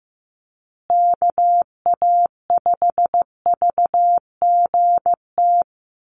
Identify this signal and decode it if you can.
Morse code, heard as a keyed tone, about 15 wpm, 700 Hz. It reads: KA5VGT
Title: USA Call Signs Repeated 3x 15wpm